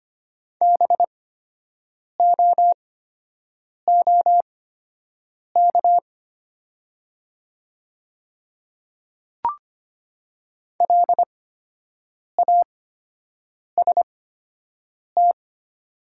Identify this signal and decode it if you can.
Morse code, heard as a keyed tone, about 25 wpm, spaced as for 7 wpm, 700 Hz. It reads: BOOK LAST
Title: 8x Character Spacing - Top 500 Words - 25wpm